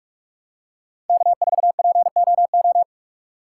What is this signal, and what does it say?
Morse code, heard as a keyed tone, about 43 wpm, 700 Hz. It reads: K4POO
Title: USA Call Signs - 5 Characters - 4x Character Spacing - 45wpm